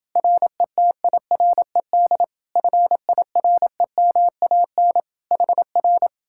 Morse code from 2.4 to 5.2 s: FIREMAN

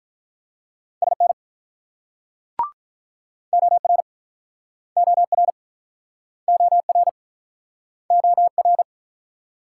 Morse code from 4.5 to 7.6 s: OR OR

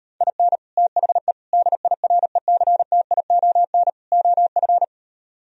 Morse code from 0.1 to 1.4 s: IN THE

